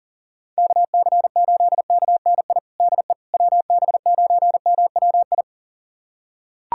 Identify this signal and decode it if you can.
Morse code, heard as a keyed tone, about 40 wpm, 700 Hz. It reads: KC8KNI DE WB9MWI
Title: USA Callsign DE Callsign 40wpm